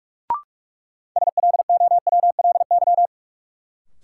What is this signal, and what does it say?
Morse code, heard as a keyed tone, about 45 wpm, 700 Hz. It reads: SLOWLY
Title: Sentences from Top 1100 Words - New Word Drill - 45wpm